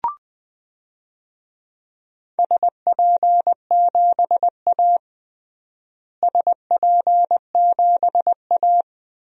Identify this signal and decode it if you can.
Morse code, heard as a keyed tone, about 20 wpm, 700 Hz. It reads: SP7A SP7A
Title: DXCC-100 Call Signs - 4 Characters - Repeated 2x - 20wpm